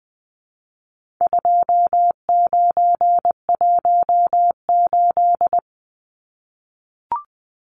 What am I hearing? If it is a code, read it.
Morse code, heard as a keyed tone, about 20 wpm, 700 Hz. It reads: 2918
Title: Numbers - 4-Digit 20wpm